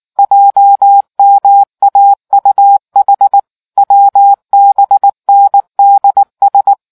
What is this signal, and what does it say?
Morse code, heard as a keyed tone, about 19 wpm, 800 Hz. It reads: JMAUH WBNDS